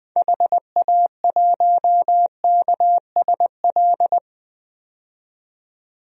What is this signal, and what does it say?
Morse code, heard as a keyed tone, about 20 wpm, 700 Hz. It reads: HA1KSL
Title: DXCC-100 Call Signs - 6 Characters - 20wpm